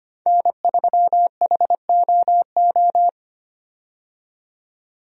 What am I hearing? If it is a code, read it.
Morse code, heard as a keyed tone, about 25 wpm, 690 Hz. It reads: N3HOO
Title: USA Call Signs - 5 Characters - 25wpm